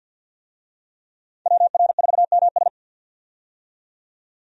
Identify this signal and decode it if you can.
Morse code, heard as a keyed tone, about 48 wpm, 700 Hz. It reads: WR4MS
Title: USA Call Signs - 5 Characters - 50wpm